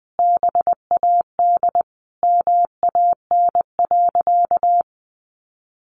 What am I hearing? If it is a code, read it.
Morse code, heard as a keyed tone, about 20 wpm, 700 Hz. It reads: BAD MAN.
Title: Sentences from Top 1200 Words 20wpm